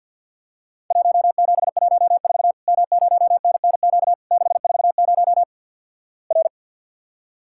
Morse code, heard as a keyed tone, about 50 wpm, 700 Hz. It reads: 1714 M0NNQ 640 R
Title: Worked All Europe DX Contest - QTC Exchange - 50wpm